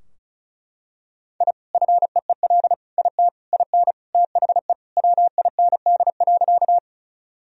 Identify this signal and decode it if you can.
Morse code, heard as a keyed tone, about 34 wpm, 700 Hz. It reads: I FEEL IT IN THE WIND.
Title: Sentences from Top 1000 Words - Review Part 2 - 35wpm